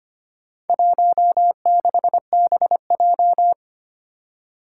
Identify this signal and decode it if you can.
Morse code, heard as a keyed tone, about 25 wpm, 700 Hz. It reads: 16BJ
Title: Four Challenging Letter-Number Combinations 25wpm